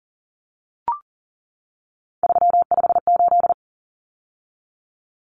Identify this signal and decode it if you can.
Morse code, heard as a keyed tone, about 40 wpm, 700 Hz. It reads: 358
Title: Common Cut Numbers - 3-Digit 40wpm